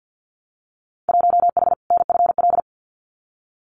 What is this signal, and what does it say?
Morse code, heard as a keyed tone, about 50 wpm, 700 Hz. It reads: 1H NFL
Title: Winter Field Day Exchanges 50wpm